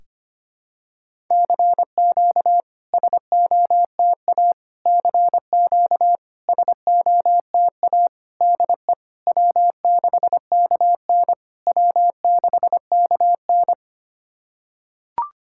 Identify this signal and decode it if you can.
Morse code, heard as a keyed tone, about 25 wpm, 700 Hz. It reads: CQ SOTA CQ SOTA DE W6KN W6KN
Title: SOTA QSO Practice 25wpm